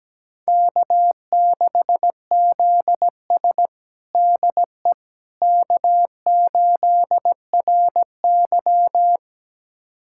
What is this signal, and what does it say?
Morse code, heard as a keyed tone, about 17 wpm, 700 Hz. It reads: K6ZS DE K8RY
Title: USA Callsign DE Callsign - 4 Characters - 17wpm